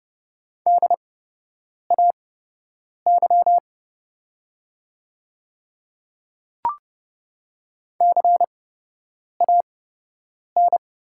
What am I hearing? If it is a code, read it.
Morse code, heard as a keyed tone, about 30 wpm, spaced as for 8 wpm, 700 Hz. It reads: DAY CAN
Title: 8x Character Spacing - Top 200 Words - 30wpm